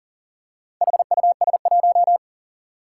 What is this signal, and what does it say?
Morse code, heard as a keyed tone, about 40 wpm, 700 Hz. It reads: HUS1
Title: Four Challenging Letter-Number Combinations 40wpm